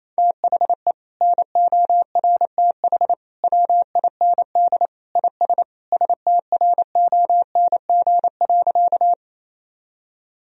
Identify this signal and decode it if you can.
Morse code, heard as a keyed tone, about 27 wpm, 700 Hz. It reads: THE NORTH WIND IS STRONG.
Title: Sentences from Top 1000 Words - Review Part 2 - 28wpm